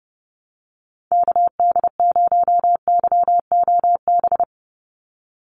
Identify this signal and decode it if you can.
Morse code, heard as a keyed tone, about 30 wpm, 700 Hz. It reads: KD0YOB
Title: NA SOTA Activator Callsigns 30wpm